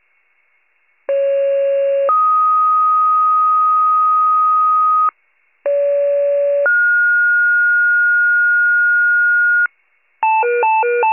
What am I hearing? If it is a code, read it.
Two-tone page, heard as a one-second tone, then a three-second tone, about 550 then 1250 Hz, about 570 then 1430 Hz, with noise. It is heard from 1.1 to 9.7 s.